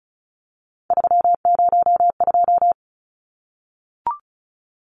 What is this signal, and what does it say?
Morse code, heard as a keyed tone, about 34 wpm, 700 Hz. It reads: 302
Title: Numbers - 3-Digit 35wpm